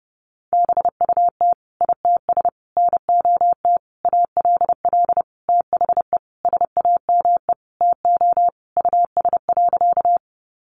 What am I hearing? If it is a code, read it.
Morse code, heard as a keyed tone, about 30 wpm, 700 Hz. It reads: BUT ITS NOT ALL THE SAME TO US.
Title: Sentences from Top 600 Words 30wpm